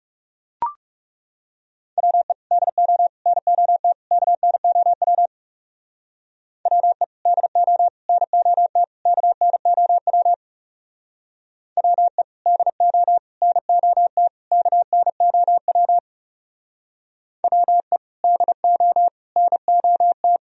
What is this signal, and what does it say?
Morse code, heard as a keyed tone, about 35 wpm, 700 Hz. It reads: WE DO NOT KNOW WE DO NOT KNOW WE DO NOT KNOW WE DO NOT
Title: Speed-Racing - Sets of 4 Words 30wpm